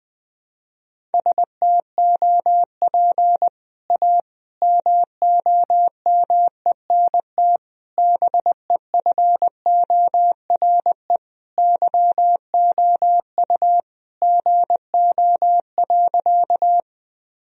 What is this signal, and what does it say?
Morse code, heard as a keyed tone, about 20 wpm, 700 Hz. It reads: STOP A MOMENT BEFORE YOU GO.
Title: Sentences from Top 1000 Words - Review Part 1 - 20wpm